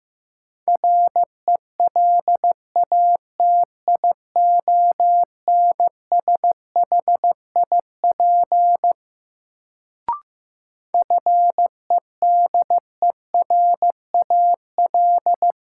Morse code, heard as a keyed tone, about 15 wpm, 700 Hz. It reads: RELATIONSHIP FEDERAL
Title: Sentences from Top 1100 Words - New Word Drill Alt - 15wpm